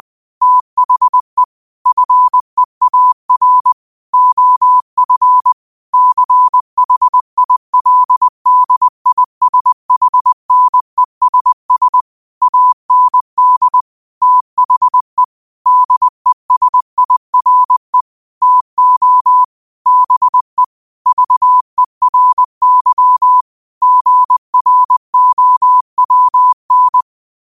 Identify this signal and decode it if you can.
Morse code, heard as a keyed tone, about 20 wpm, 1000 Hz. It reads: THE FEAR OF CHILDISHNESS AND THE DESIRE TO BE VERY GROWN
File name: 20wpm